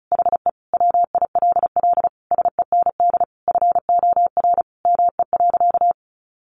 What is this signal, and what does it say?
Morse code, heard as a keyed tone, about 34 wpm, 700 Hz. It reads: HE WILL SEND FOR ME.
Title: Sentences from Top 1500 Words 35wpm